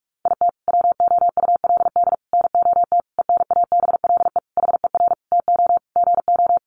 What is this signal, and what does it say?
Morse code, heard as a keyed tone, about 44 wpm, 700 Hz. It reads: IT WOULD NOT ENABLE HER TO GO
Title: Sentences from Top 1500 Words 45wpm